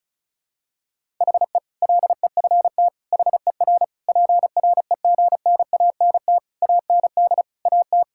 Morse code, heard as a keyed tone, about 36 wpm, 700 Hz. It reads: HE LEFT HER PREGNANT AND AT
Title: Sentences from Top 1000 Words - Review Part 6 - 35wpm